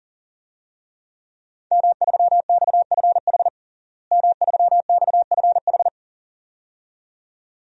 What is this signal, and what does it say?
Morse code, heard as a keyed tone, about 40 wpm, 700 Hz. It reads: M3XFH M3XFH
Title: DXCC-100 Call Signs - 5 Characters - Repeated 2x - 40wpm